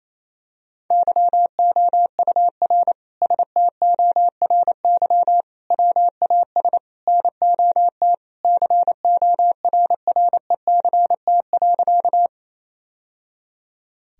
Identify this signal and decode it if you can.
Morse code, heard as a keyed tone, about 28 wpm, 700 Hz. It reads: YOUR STORY WAS NOT CORRECT.
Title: Sentences from Top 1000 Words - Review Part 2 - 28wpm